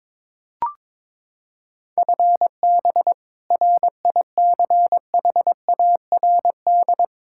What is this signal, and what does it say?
Morse code, heard as a keyed tone, about 22 wpm, 700 Hz. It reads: FB RICHARD
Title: Practice QSO - Exchange 3 - More Info - 22wpm